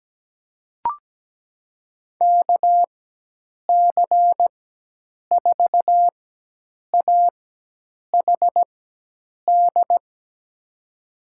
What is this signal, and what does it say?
Morse code, heard as a keyed tone, about 17 wpm, spaced as for 8 wpm, 700 Hz. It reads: KC4AHD
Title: USA Call Signs - 6 Characters - 4x Character Spacing - 17wpm